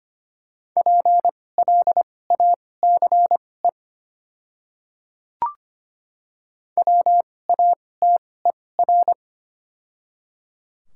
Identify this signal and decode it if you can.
Morse code, heard as a keyed tone, about 25 wpm, spaced as for 18 wpm, 690 Hz. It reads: PLACE WATER
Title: 2x Character Spacing - Top 200 Words - 25wpm